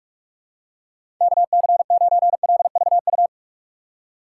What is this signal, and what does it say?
Morse code, heard as a keyed tone, about 43 wpm, 700 Hz. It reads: KC9LVU
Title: USA Call Signs - 6 Characters - 2x Character Spacing - 45wpm